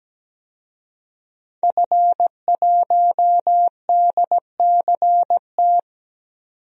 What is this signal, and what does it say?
Morse code, heard as a keyed tone, about 17 wpm, 700 Hz. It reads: F1DCT
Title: DXCC-100 Call Signs - 5 Characters - 17wpm